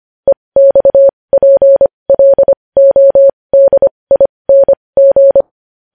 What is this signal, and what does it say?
Morse code, heard as a keyed tone, about 25 wpm, spaced as for 20 wpm, 550 Hz. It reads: EXPLODING
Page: News headlines in morse code at 20 wpm